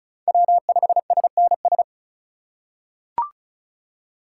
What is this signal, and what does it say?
Morse code, heard as a keyed tone, about 36 wpm, 700 Hz. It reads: W5SNS
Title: USA Call Signs - 8x Character Spacing - 35wpm